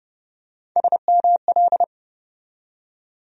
Morse code, heard as a keyed tone, about 30 wpm, 700 Hz. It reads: SML
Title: USA Call Sign Postfixes - 3 Letters 30wpm